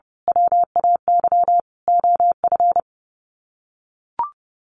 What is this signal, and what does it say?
Morse code, heard as a keyed tone, about 30 wpm, 700 Hz. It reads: WAY OF
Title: Sets of 4 Words 30wpm